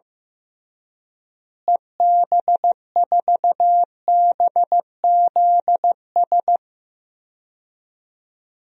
Morse code, heard as a keyed tone, about 15 wpm, 700 Hz. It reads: EB4BZS
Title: DXCC-100 Call Signs - 6 Characters - Repeated 3x - 15wpm